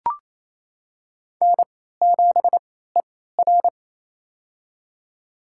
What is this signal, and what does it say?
Morse code, heard as a keyed tone, about 27 wpm, spaced as for 16 wpm, 700 Hz. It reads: N7ER
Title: USA Call Signs - 4 Characters - 3x Character Spacing - 28wpm